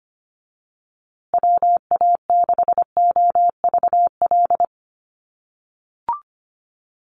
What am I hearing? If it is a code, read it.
Morse code, heard as a keyed tone, about 25 wpm, 700 Hz. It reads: WA6OVL
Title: USA Call Signs Repeated 2x 25wpm